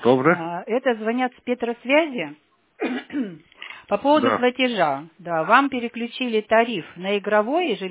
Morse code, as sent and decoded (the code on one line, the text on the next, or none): none